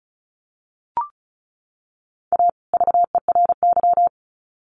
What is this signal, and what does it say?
Morse code, heard as a keyed tone, about 34 wpm, 700 Hz. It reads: A VERY
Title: Sets of 2 Words 35wpm